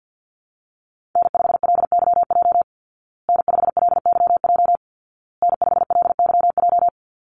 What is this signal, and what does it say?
Morse code, heard as a keyed tone, about 50 wpm, 700 Hz. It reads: N5LYJ N5LYJ N5LYJ
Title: Call Signs Repeated 2x - 50wpm